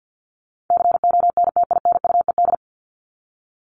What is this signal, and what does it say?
Morse code, heard as a keyed tone, about 50 wpm, 700 Hz. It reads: CONTINUED
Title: ED Suffix - Top 100 Words with Base Words 50wpm